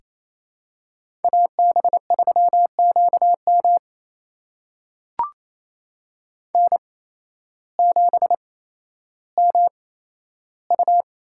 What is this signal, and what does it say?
Morse code, heard as a keyed tone, about 28 wpm, spaced as for 13 wpm, 700 Hz. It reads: AB3QM N7MU
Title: USA Call Signs - 8x Character Spacing - 28wpm